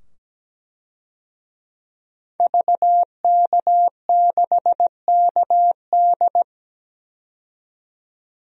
Morse code, heard as a keyed tone, about 17 wpm, 700 Hz. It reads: VK6KD
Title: DXCC-100 Call Signs - 5 Characters - 17wpm